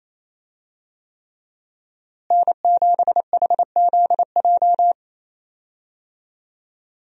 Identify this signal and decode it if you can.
Morse code, heard as a keyed tone, about 28 wpm, 700 Hz. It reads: N7HZJ
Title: USA Call Signs - 5 Characters - Repeated 3x - 28wpm